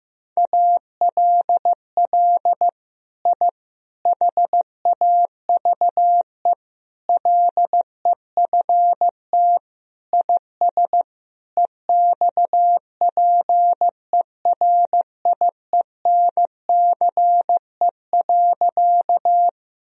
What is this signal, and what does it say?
Morse code, heard as a keyed tone, about 16 wpm, 700 Hz. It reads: ALL I HAVE LEFT IS EXPERIENCE.
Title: Sentences from Top 1000 Words - Review Part 3 - 15wpm